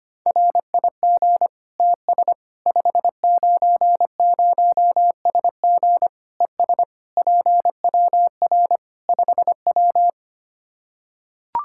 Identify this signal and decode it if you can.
Morse code, heard as a keyed tone, about 25 wpm, 700 Hz. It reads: RIG TS 590SG ES PWR 5W
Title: Practice QSO - Exchange 2 - Info - 25wpm